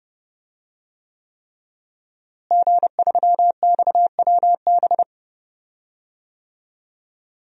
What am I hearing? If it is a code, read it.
Morse code, heard as a keyed tone, about 30 wpm, 700 Hz. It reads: G3XWB
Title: DXCC-100 Call Signs - 5 Characters - Repeated 3x - 30wpm